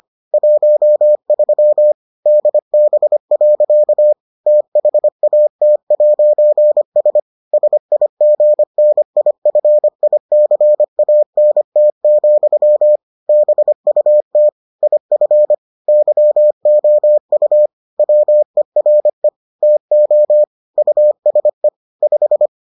Morse code, heard as a keyed tone, about 25 wpm, 600 Hz. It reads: 13 DB. THAT'S SIGNIFICANT, BUT IF YOU WERE TO USE 5